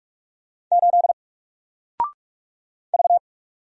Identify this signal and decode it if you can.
Morse code, heard as a keyed tone, about 46 wpm, 700 Hz. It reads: 8 V
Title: Single Letter-Number Alt 45wpm